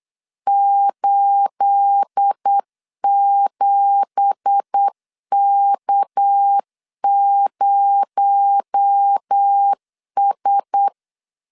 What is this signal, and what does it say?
Morse code, heard as a keyed tone, about 8 wpm, 800 Hz. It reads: 87K0S
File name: 8wpm